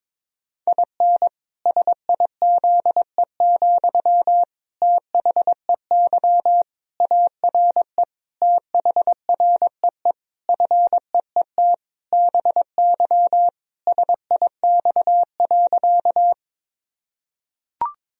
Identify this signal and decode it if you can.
Morse code, heard as a keyed tone, about 22 wpm, 700 Hz. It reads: IN SIZE, THEY ARE THREE FEET BY SIX.
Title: Sentences from Top 1000 Words - Review Part 2 - 22wpm